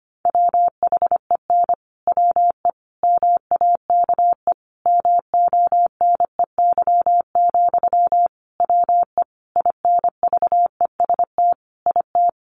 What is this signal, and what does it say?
Morse code, heard as a keyed tone, about 25 wpm, 700 Hz. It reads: WHEN WE MAKE MONEY, WE INVEST IT
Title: Sentences from Top 1100 Words 25wpm